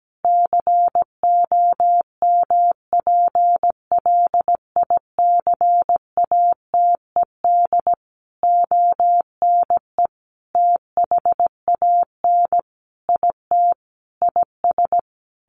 Morse code, heard as a keyed tone, about 17 wpm, 700 Hz. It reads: COMPLICATED ONE THAN IT IS
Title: Sentences from Top 1200 Words 17wpm